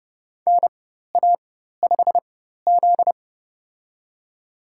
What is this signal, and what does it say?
Morse code, heard as a keyed tone, about 30 wpm, spaced as for 14 wpm, 700 Hz. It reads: NA5Z
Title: USA Call Signs - 4 Characters - 4x Character Spacing - 30wpm